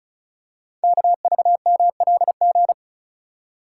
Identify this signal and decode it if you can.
Morse code, heard as a keyed tone, about 36 wpm, 700 Hz. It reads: KVMLG
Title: Code Groups - 5 Letters - 35wpm